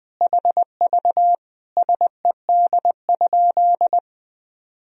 Morse code, heard as a keyed tone, about 21 wpm, 700 Hz. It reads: HV SED?
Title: Sentences from Top 100 Words and Abbreviations 20wpm